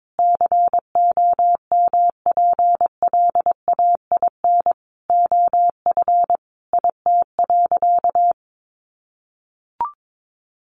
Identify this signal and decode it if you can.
Morse code, heard as a keyed tone, about 22 wpm, 700 Hz. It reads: COMPLAIN OF IT.
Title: Sentences from Top 1300 Words 22wpm